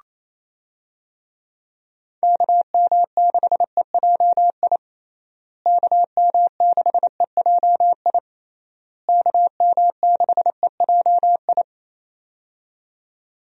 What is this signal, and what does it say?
Morse code, heard as a keyed tone, about 27 wpm, 700 Hz. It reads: KM6EJI KM6EJI KM6EJI
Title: USA Call Signs - 6 Characters - Repeated 3x - 28wpm